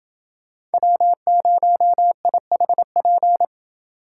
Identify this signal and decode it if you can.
Morse code, heard as a keyed tone, about 28 wpm, 700 Hz. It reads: W0IHP